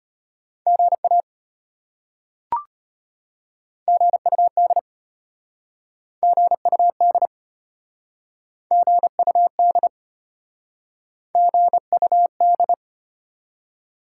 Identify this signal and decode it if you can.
Morse code, heard as a keyed tone, about 34 wpm, 700 Hz. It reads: GA GUD GUD GUD GUD